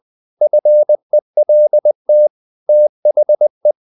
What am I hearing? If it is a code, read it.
Morse code, heard as a keyed tone, about 20 wpm, 600 Hz. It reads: FELT THE
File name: the old man and the sea part 2 20wpm